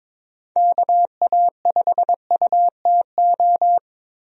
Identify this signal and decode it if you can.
Morse code, heard as a keyed tone, about 22 wpm, 700 Hz. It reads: KA5UTO
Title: USA Call Signs - 6 Characters - 22wpm